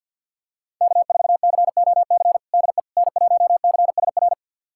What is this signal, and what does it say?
Morse code, heard as a keyed tone, about 52 wpm, 700 Hz. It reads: K4COC DE N1CSR
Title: USA Callsign DE Callsign 50wpm